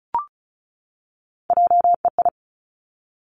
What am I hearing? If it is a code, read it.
Morse code, heard as a keyed tone, about 36 wpm, 700 Hz. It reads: JEI
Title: Three Challenging Letter-Number Combinations 35wpm